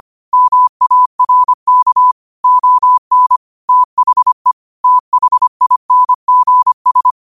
Morse code, heard as a keyed tone, about 25 wpm, 1000 Hz. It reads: MARK ON THE THINGS
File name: qotdfiles / 25wpm